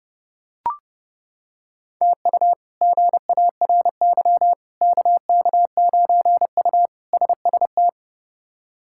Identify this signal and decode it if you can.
Morse code, heard as a keyed tone, about 30 wpm, 700 Hz. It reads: TU GARY KK9U SST
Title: K1USN SST QSO Practice - 30wpm